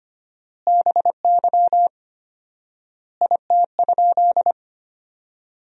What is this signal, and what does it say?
Morse code, heard as a keyed tone, about 25 wpm, 700 Hz. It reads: BY IT?
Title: 4x Word-Spacing - Sentences from Top 100 Words - 25wpm